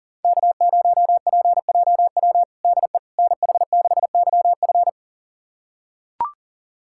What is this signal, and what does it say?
Morse code, heard as a keyed tone, about 40 wpm, 700 Hz. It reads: K0PJW DE NH6YF
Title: USA Callsign DE Callsign - 5 Characters - 40wpm